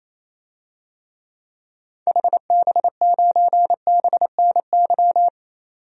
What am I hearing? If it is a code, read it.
Morse code, heard as a keyed tone, about 28 wpm, 700 Hz. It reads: HB9BNY
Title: DXCC-100 Call Signs - 6 Characters - 28wpm